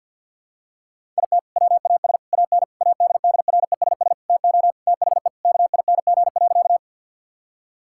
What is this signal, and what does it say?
Morse code, heard as a keyed tone, about 47 wpm, 700 Hz. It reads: IT WAS AN ADDRESS TO THE KING.
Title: Sentences from Top 1000 Words - Review Part 6 - 50wpm